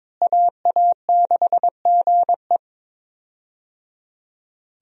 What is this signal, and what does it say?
Morse code, heard as a keyed tone, about 21 wpm, 700 Hz. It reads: AA6GE
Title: USA Call Signs - 5 Characters - Repeated 3x - 22wpm